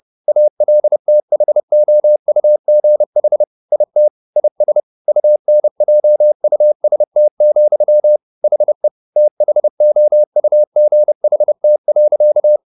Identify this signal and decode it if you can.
Morse code, heard as a keyed tone, about 30 wpm, 600 Hz. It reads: ALTHOUGH IT IS UNJUST, HE THOUGHT.
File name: the old man and the sea / the old man and the sea 1 30wpm